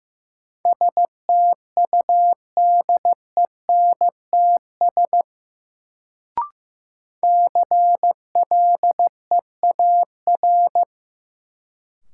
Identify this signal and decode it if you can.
Morse code, heard as a keyed tone, about 15 wpm, 700 Hz. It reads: STUDENTS CLEAR
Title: Sets of 5 Words Encore - New Word Drill - 15wpm